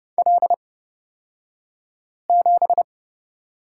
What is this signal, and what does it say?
Morse code, heard as a keyed tone, about 31 wpm, spaced as for 5 wpm, 700 Hz. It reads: L7
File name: Lesson 36d - BT - Two Characters 30wpm